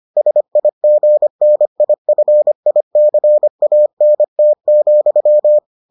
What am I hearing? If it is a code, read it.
Morse code, heard as a keyed tone, about 25 wpm, 600 Hz. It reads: SIGNIFICANT,